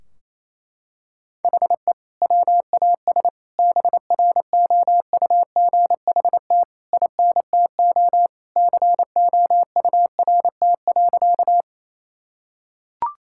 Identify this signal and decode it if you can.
Morse code, heard as a keyed tone, about 27 wpm, 700 Hz. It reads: HE WAS BROUGHT INTO COURT.
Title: Sentences from Top 1600 Words 28wpm